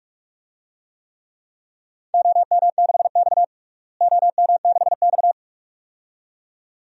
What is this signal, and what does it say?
Morse code, heard as a keyed tone, about 46 wpm, 700 Hz. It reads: OM6X OM6X
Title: DXCC-100 Call Signs - 4 Characters - Repeated 2x - 45wpm